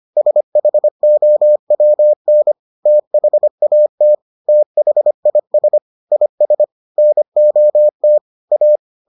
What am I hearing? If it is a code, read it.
Morse code, heard as a keyed tone, about 25 wpm, 600 Hz. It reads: SHOWN THAT THIS IS NOT A